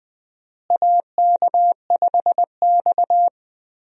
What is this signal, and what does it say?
Morse code, heard as a keyed tone, about 20 wpm, 700 Hz. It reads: AK5X
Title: USA Call Signs - 4 Characters - Repeated 2x - 20wpm